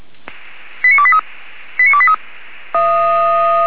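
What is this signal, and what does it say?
Five-tone selective call (ZVEI), heard as quick runs of five tones, tones of about 70 ms. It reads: 89283 89283
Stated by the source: ZVEI five-tone selective call 89247, received by SDR.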